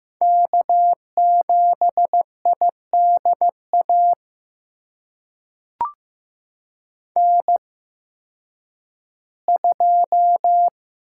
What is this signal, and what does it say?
Morse code, heard as a keyed tone, about 15 wpm, spaced as for 10 wpm, 700 Hz. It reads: K7IDA N2
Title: USA Call Signs - 5 Characters - 8x Character Spacing - 15wpm